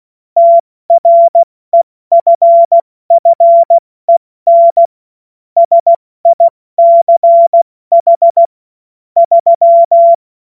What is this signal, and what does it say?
Morse code, heard as a keyed tone, about 16 wpm, 680 Hz. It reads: TREFFEN SICH 3
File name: Morsekurs / Witze / Witz 01. Februar 2022 70 F80